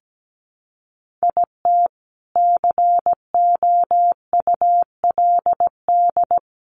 Morse code, heard as a keyed tone, about 17 wpm, 700 Hz. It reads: IT COULD
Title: A Sound Of Thunder 17wpm